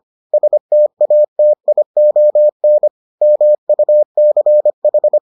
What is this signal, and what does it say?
Morse code, heard as a keyed tone, about 25 wpm, 600 Hz. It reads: STATION MUCH